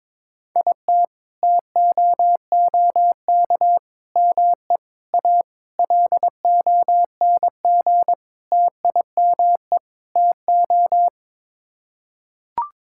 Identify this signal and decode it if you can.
Morse code, heard as a keyed tone, about 22 wpm, 700 Hz. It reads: IT TOOK ME A LONG TIME TO
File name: practice / 22wpm / Sets of 7 Words - 22wpm